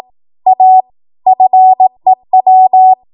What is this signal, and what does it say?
Morse code, heard as a keyed tone, about 18 wpm, 750 Hz. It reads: A FEW
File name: Archive / 18 WPM / 18WPM